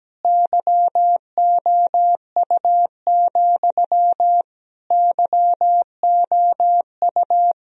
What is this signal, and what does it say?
Morse code, heard as a keyed tone, about 17 wpm, 700 Hz. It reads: YOU, YOU
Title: Sentences from Top 1000 Words - Review Part 3 - 17wpm